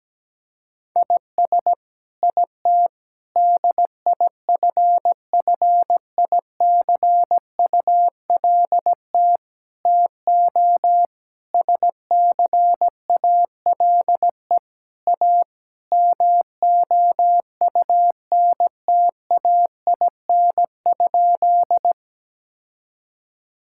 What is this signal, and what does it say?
Morse code, heard as a keyed tone, about 17 wpm, 700 Hz. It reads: IS IT DIFFICULT TO SCALE A MOUNTAIN?
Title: Sentences from Top 1000 Words - Review Part 5 - 17wpm